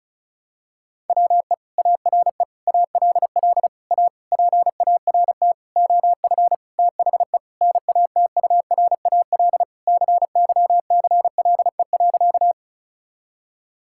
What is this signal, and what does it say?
Morse code, heard as a keyed tone, about 34 wpm, 700 Hz. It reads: WE ARE ALL A PART OF THE NATURAL CYCLE.
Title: Sentences from Top 1000 Words - Review Part 6 - 35wpm